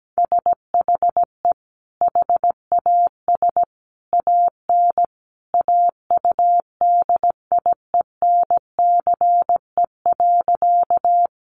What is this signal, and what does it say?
Morse code, heard as a keyed tone, about 17 wpm, 700 Hz. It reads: SHE HAS AN AUDIENCE.
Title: Sentences from Top 800 Words 17wpm